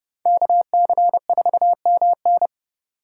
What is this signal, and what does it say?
Morse code, heard as a keyed tone, about 31 wpm, 700 Hz. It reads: KC4MN
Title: USA Call Signs - 5 Characters - 6x Character Spacing - 30wpm